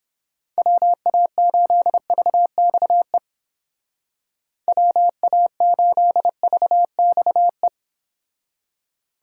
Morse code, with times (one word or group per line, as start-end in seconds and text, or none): 0.6-3.2 WA8VXE
4.7-7.7 WA8VXE